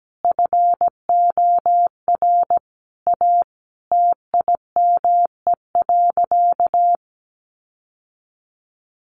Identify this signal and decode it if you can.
Morse code, heard as a keyed tone, about 17 wpm, 700 Hz. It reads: FOR A TIME.
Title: Sentences from Top 700 Words 17wpm